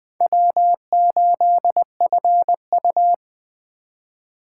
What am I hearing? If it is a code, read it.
Morse code, heard as a keyed tone, about 20 wpm, 700 Hz. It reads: W8FU